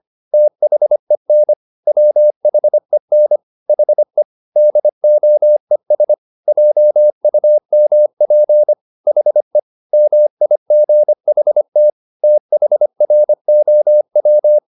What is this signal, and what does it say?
Morse code, heard as a keyed tone, about 25 wpm, 600 Hz. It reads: THEN WHEN HE DOES JUMP HE MIGHT THROW
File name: the old man and the sea part 1 25wpm